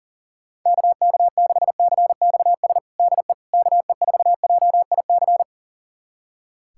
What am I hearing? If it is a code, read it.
Morse code, heard as a keyed tone, about 37 wpm, 700 Hz. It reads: KK6CXS DE KE4JIC
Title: USA Callsign DE Callsign - 6 Characters - 40wpm